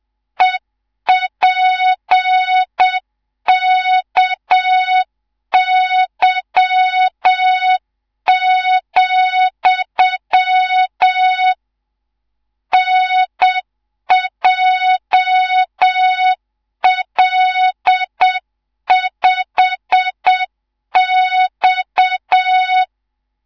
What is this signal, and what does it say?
Morse code, heard as a keyed tone, about 7 wpm, 3700 Hz, with noise. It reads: EPKY, NJL5X
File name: Morze-HA2MN / hang / 19 GYAKORLAT